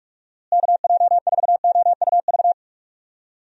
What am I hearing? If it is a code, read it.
Morse code, heard as a keyed tone, about 46 wpm, 700 Hz. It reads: KJ4OUV
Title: USA Call Signs - 6 Characters - Repeated 3x - 45wpm